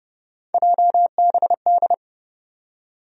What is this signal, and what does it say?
Morse code, heard as a keyed tone, about 30 wpm, 700 Hz. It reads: JBD
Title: USA Call Sign Postfixes - 3 Letters 30wpm